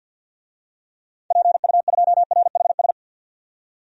Morse code, heard as a keyed tone, about 46 wpm, 700 Hz. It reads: PU2RHS
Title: DXCC-100 Call Signs - 6 Characters - 50wpm